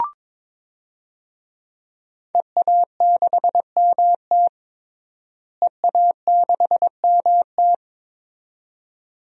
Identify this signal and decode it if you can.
Morse code, heard as a keyed tone, about 22 wpm, 700 Hz. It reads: EA6MT EA6MT
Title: DXCC-100 Call Signs - 5 Characters - Repeated 2x - 22wpm